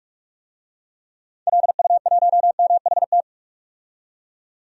Morse code, heard as a keyed tone, about 46 wpm, 700 Hz. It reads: LU1MHT
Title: DXCC-100 Call Signs - 6 Characters - 45wpm